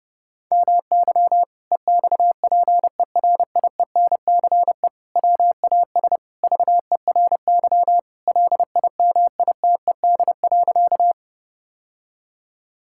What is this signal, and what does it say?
Morse code, heard as a keyed tone, about 32 wpm, 700 Hz. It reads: MY EXPERIENCE WAS VERY LIMITED.
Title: Sentences from Top 1000 Words - Review Part 6 - 30wpm